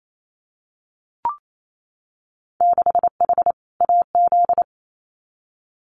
Morse code, heard as a keyed tone, about 28 wpm, 700 Hz. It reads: 6H AZ
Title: Winter Field Day Exchanges Encore 28wpm